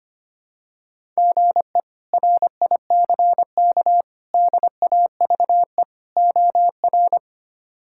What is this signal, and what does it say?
Morse code, heard as a keyed tone, about 25 wpm, 700 Hz. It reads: GE RICK DAVE OR
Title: K1USN SST QSO Practice - 25wpm